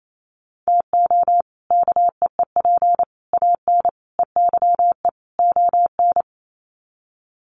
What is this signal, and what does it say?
Morse code, heard as a keyed tone, about 28 wpm, 700 Hz. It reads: TO KEEP AN EYE ON
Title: Sets of 5 Words Encore 28wpm